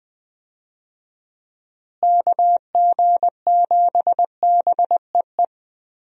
Morse code, heard as a keyed tone, about 20 wpm, 700 Hz. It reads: KG7BEE
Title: USA Call Signs - 6 Characters - Repeated 3x - 20wpm